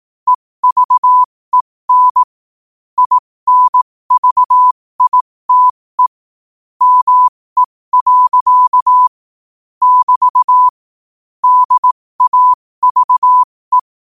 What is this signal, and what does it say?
Morse code, heard as a keyed tone, about 18 wpm, spaced as for 15 wpm, 1000 Hz. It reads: EVEN INVITE ME. <BT> DAVE
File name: 15wpm